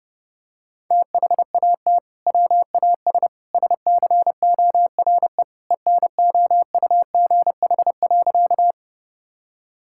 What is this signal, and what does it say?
Morse code, heard as a keyed tone, about 30 wpm, 700 Hz. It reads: THAT WAS SCORE ENOUGH.
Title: Sentences from Top 1600 Words 30wpm